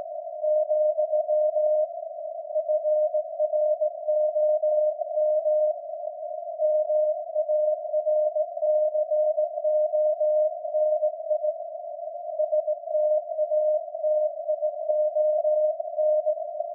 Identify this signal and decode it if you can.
Morse code, heard as a keyed tone, about 17 wpm, 610 Hz. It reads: , FROM MARCONI STATION
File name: iz7slz / VLF / SAQ jun 2019